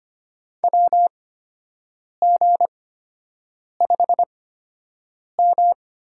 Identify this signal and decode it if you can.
Morse code, heard as a keyed tone, about 25 wpm, spaced as for 7 wpm, 700 Hz. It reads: WG5M
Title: USA Call Signs - 8x Character Spacing - 25wpm